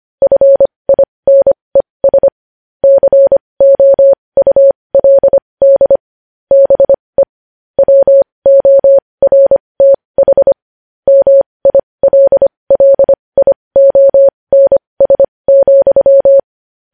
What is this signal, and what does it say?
Morse code, heard as a keyed tone, about 25 wpm, spaced as for 20 wpm, 550 Hz. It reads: FINES COULD BE WORTH MILLIONS,
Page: News Headlines in Morse Code at 20 WPM